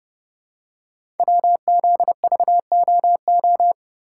Morse code, heard as a keyed tone, about 30 wpm, 700 Hz. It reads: WZVOO
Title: Code Groups - 5 Letter:Numbers - 30wpm